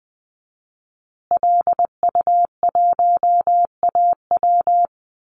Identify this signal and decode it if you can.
Morse code, heard as a keyed tone, about 20 wpm, 700 Hz. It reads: LU1AW
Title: Call Signs Repeated 2x - 20wpm